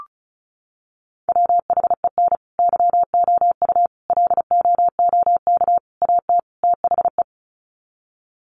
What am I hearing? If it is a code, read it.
Morse code, heard as a keyed tone, about 34 wpm, 700 Hz. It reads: WHEN YOU LOOK AT THE
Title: Sets of 5 Words 35wpm